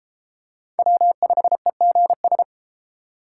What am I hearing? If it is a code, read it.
Morse code, heard as a keyed tone, about 31 wpm, 700 Hz. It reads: W5EGS